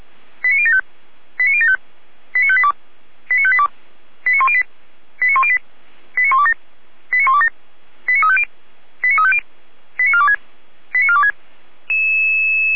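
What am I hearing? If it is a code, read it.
Five-tone selective call (ZVEI), heard as quick runs of five tones, tones of about 80 ms. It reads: 89075 89075 89582 89582 89108 89108 89127 89127 89350 89350 89426 89426
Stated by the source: Five-tone ZVEI selective call received by SDR.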